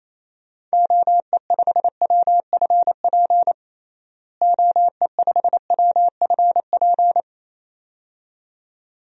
Morse code, heard as a keyed tone, about 28 wpm, 700 Hz. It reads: OE5WFP OE5WFP
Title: DXCC-100 Call Signs - 6 Characters - Repeated 2x - 28wpm